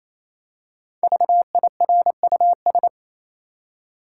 Morse code, heard as a keyed tone, about 28 wpm, 700 Hz. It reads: VIRUS
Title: Five Letter Words - 28wpm